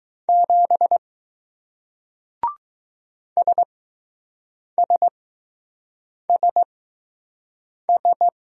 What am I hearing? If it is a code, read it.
Morse code, heard as a keyed tone, about 20 wpm, spaced as for 6 wpm, 700 Hz. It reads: 7 SSSS